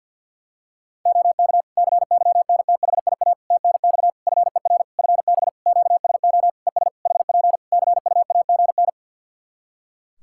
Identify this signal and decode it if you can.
Morse code, heard as a keyed tone, about 49 wpm, 700 Hz. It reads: OK CYNTHIA TNX FER FB QSO ES HP CUAGN
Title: K1USN SST QSO Practice 50wpm